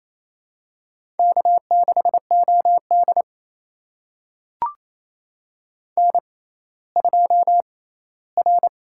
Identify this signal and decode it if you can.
Morse code, heard as a keyed tone, about 28 wpm, spaced as for 16 wpm, 700 Hz. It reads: K6OD N2R
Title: USA Call Signs - 4 Characters - 6x Character Spacing - 28wpm